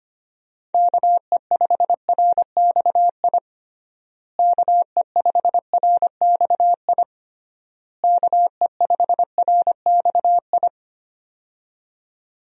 Morse code, heard as a keyed tone, about 25 wpm, 700 Hz. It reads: KE5RXI KE5RXI KE5RXI
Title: USA Call Signs - 6 Characters - Repeated 3x - 25wpm